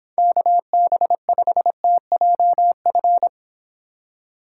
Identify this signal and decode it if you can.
Morse code, heard as a keyed tone, about 25 wpm, 700 Hz. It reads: KB5TJF